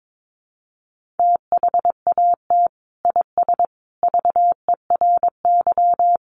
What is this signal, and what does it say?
Morse code, heard as a keyed tone, about 22 wpm, 700 Hz. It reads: THAT IS VERY
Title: Sentences from Top 400 Words 22wpm